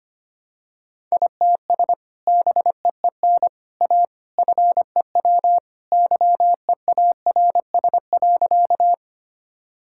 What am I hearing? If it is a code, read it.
Morse code, heard as a keyed tone, about 25 wpm, 700 Hz. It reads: ITS BEEN A FEW YEARS.
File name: practice / 25wpm / Sentences with 3-Word Phrases - 25wpm